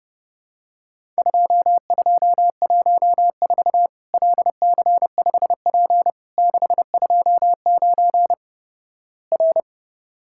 Morse code, heard as a keyed tone, about 32 wpm, 700 Hz. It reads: 2214 LC5P 629 R
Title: Worked All Europe DX Contest - QTC Exchange - 30wpm